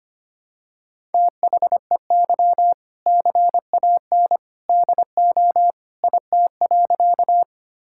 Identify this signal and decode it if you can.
Morse code, heard as a keyed tone, about 25 wpm, 700 Hz. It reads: THEY CAN DO IT.
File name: practice / 25wpm / Sentences with 2-Word Phrases - 25wpm